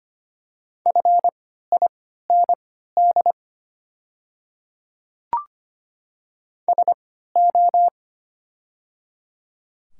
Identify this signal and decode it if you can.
Morse code, heard as a keyed tone, about 25 wpm, spaced as for 14 wpm, 700 Hz. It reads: FIND SO
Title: 3x Character Spacing - Top 200 Words - 25wpm